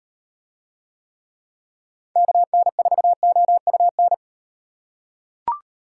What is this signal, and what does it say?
Morse code, heard as a keyed tone, about 35 wpm, 700 Hz. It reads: KN4OUN